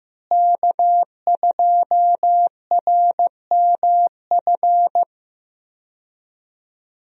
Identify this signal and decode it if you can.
Morse code, heard as a keyed tone, about 15 wpm, 700 Hz. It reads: K2RMF
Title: USA Call Signs - 5 Characters - Repeated 2x - 15wpm